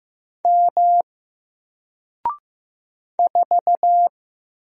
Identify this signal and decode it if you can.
Morse code, heard as a keyed tone, about 15 wpm, 700 Hz. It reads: M 4
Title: Single Letter-Number 15wpm